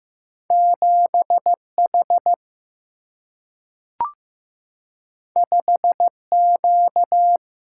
Morse code, heard as a keyed tone, about 15 wpm, 700 Hz. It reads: 7 H 5 Q
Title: Single-Letter Single-Number 15wpm